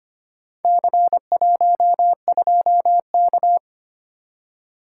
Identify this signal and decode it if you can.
Morse code, heard as a keyed tone, about 25 wpm, 700 Hz. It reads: C12K
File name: practice / 25wpm / Four Letter-Number Combinations - 25wpm